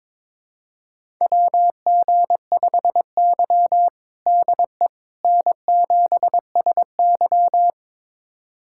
Morse code, heard as a keyed tone, about 22 wpm, 700 Hz. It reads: WG5Y DE N7SY
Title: USA Callsign DE Callsign - 4 Characters - 22wpm